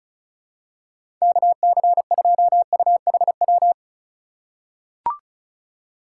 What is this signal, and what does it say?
Morse code, heard as a keyed tone, about 36 wpm, 700 Hz. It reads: KC2UHW
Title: USA Call Signs - 8x Character Spacing - 35wpm